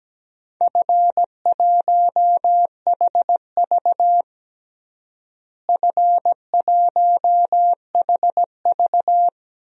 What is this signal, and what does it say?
Morse code, heard as a keyed tone, about 17 wpm, 700 Hz. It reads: F1HV F1HV
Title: DXCC-100 Call Signs - 4 Characters - Repeated 3x - 17wpm